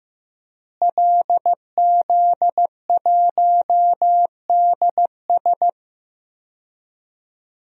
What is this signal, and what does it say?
Morse code, heard as a keyed tone, about 15 wpm, 700 Hz. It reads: LZ1DS
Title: DXCC-100 Call Signs - 5 Characters - Repeated 2x - 15wpm